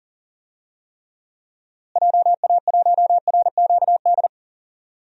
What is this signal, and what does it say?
Morse code, heard as a keyed tone, about 40 wpm, 700 Hz. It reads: JA1RQD
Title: DXCC-100 Call Signs - 6 Characters - Repeated 3x - 40wpm